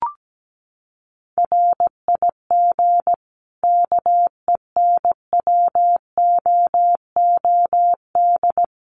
Morse code, heard as a keyed tone, about 17 wpm, 700 Hz. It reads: RIG KENWOOD
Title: Common QSO Phrases 17wpm